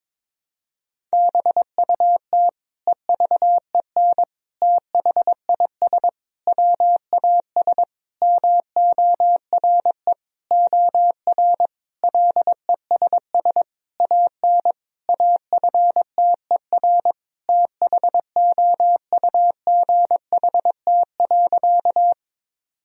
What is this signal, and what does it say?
Morse code, heard as a keyed tone, about 22 wpm, 700 Hz. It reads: BUT EVEN THIS WAS MORE OR LESS AN AFTER THOUGHT.
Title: Sentences from Top 1000 Words - Review Part 2 - 22wpm